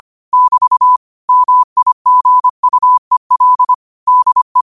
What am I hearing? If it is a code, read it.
Morse code, heard as a keyed tone, about 25 wpm, 1000 Hz. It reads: <BT> MIGUEL DE
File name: qotdfiles / 25wpm